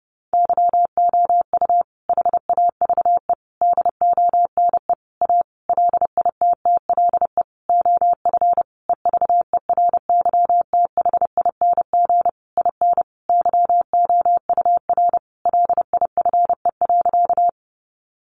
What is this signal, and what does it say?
Morse code, heard as a keyed tone, about 30 wpm, 700 Hz. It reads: YOU HAVE DONE A LITTLE OF EVERYTHING IN YOUR LIFE.
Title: Sentences from Top 700 Words 30wpm